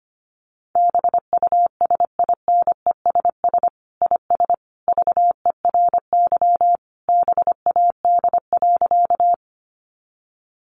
Morse code, heard as a keyed tone, about 25 wpm, 700 Hz. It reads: BUSINESS IS VERY BAD.
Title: Sentences from Top 700 Words 25wpm